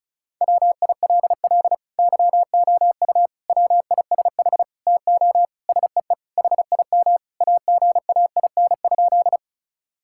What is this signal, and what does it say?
Morse code, heard as a keyed tone, about 34 wpm, 700 Hz. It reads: WILL YOU WISH TO SEE HIM AGAIN?
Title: Sentences from Top 1600 Words 35wpm